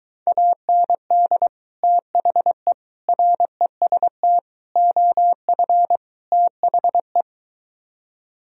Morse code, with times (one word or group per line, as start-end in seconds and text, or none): 0.3-1.5 AND
1.8-2.7 THE
3.1-4.4 REST
4.8-6.0 OF
6.3-7.2 THE